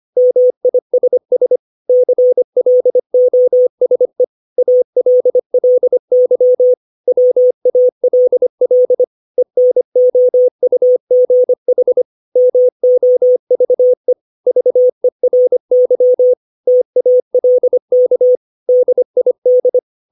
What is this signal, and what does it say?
Morse code, heard as a keyed tone, about 25 wpm, 500 Hz. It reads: MISS CLOSE ALLY WALL ENOUGH MOVE VERY TALK DID
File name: 632 words / 632 Words 25WPM